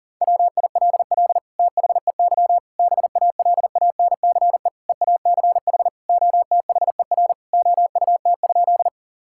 Morse code, heard as a keyed tone, about 40 wpm, 690 Hz. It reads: WILL THEY BALANCE EACH OTHER OUT?
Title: Sentences from Top 1000 Words - Review Part 6 - 40wpm